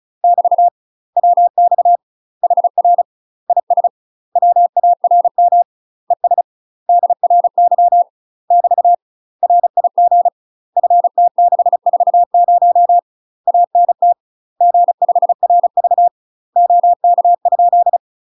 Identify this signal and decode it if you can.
Morse code, heard as a keyed tone, about 35 wpm, 700 Hz. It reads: <BT> WX HR IS WARM ES DRY <BT> RIG FT640 ANT G5RV OK?
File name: UK MASTER QSOs TRAINING / 35WPM / UK MASTER QSOs 0013